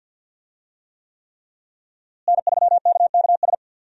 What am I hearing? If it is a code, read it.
Morse code, heard as a keyed tone, about 50 wpm, 700 Hz. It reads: N3KKS
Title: USA Call Signs - 5 Characters - Repeated 3x - 50wpm